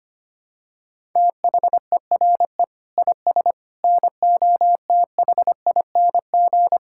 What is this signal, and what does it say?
Morse code, heard as a keyed tone, about 25 wpm, 700 Hz. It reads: THERE IS NOTHING
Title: Sentences from Top 1000 Words - Review Part 5 - 25wpm